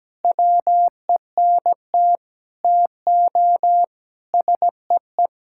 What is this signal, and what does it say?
Morse code, heard as a keyed tone, about 17 wpm, 700 Hz. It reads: WENT TO SEE
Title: Sentences from Top 1000 Words - Review Part 5 - 17wpm